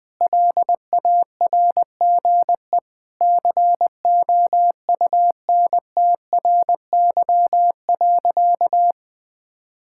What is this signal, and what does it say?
Morse code, heard as a keyed tone, about 20 wpm, 700 Hz. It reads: LARGE COUNTRY.
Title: Sentences with 4-Word Phrases - 20wpm